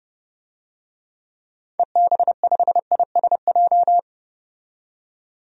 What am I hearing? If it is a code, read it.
Morse code, heard as a keyed tone, about 30 wpm, 690 Hz. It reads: EB5ISJ
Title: DXCC-100 Call Signs - 6 Characters - 30wpm